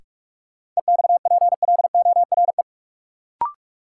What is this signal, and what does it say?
Morse code, heard as a keyed tone, about 47 wpm, 700 Hz. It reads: EXPLORE
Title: Sentences from Top 1100 Words - New Word Drill - 45wpm